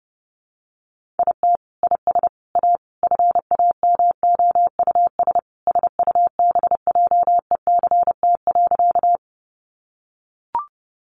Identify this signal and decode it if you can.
Morse code, heard as a keyed tone, about 30 wpm, 700 Hz. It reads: IT IS A FAMOUS SUBJECT.
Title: Sentences from Top 1500 Words 30wpm